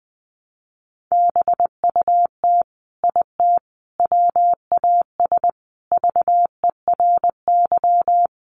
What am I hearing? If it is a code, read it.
Morse code, heard as a keyed tone, about 20 wpm, 700 Hz. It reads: BUT IT WAS VERY
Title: Sentences from Top 300 Words 20wpm